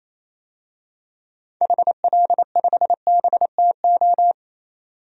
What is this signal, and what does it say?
Morse code, heard as a keyed tone, about 28 wpm, 700 Hz. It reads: HL5BTO
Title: DXCC-100 Call Signs - 6 Characters - 28wpm